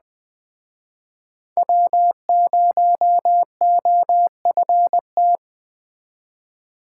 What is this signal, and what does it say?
Morse code, heard as a keyed tone, about 20 wpm, 700 Hz. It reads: W0OFT
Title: Speed Racing - USA 1x3 Call Signs 17wpm